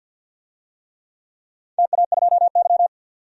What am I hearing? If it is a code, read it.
Morse code, heard as a keyed tone, about 48 wpm, 700 Hz. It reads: TA2Y
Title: DXCC-100 Call Signs - 4 Characters - 50wpm